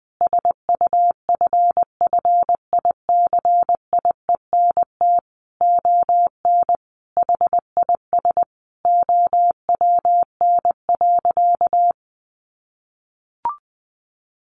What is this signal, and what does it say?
Morse code, heard as a keyed tone, about 20 wpm, 700 Hz. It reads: SUFFICIENT ON HIS OWN.
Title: Sentences from Top 1400 Words 20wpm